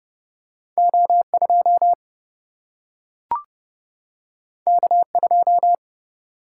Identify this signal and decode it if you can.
Morse code, heard as a keyed tone, about 30 wpm, 700 Hz. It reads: O 2 K 2
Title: Single-Letter Single-Number 30wpm